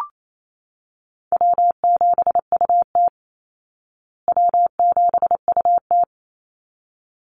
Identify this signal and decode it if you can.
Morse code, heard as a keyed tone, about 28 wpm, 690 Hz. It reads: W7UT W7UT
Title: Call Signs Repeated 3x - 28wpm